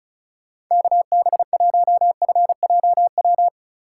Morse code, heard as a keyed tone, about 36 wpm, 700 Hz. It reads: KB1FJW
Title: USA Call Signs - 6 Characters - 4x Character Spacing - 35wpm